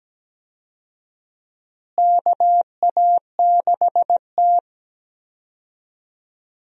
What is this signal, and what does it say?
Morse code, heard as a keyed tone, about 17 wpm, 700 Hz. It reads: KA6T
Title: USA Call Signs - 4 Characters - Repeated 2x - 17wpm